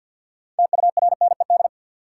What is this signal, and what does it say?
Morse code, heard as a keyed tone, about 51 wpm, 700 Hz. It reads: TURNED
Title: Sets of 4 Words - New Word Drill - 50wpm